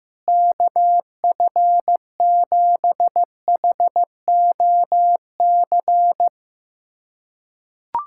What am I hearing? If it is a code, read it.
Morse code, heard as a keyed tone, about 16 wpm, 700 Hz. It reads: KF7HOC
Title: USA Callsign DE Callsign 15wpm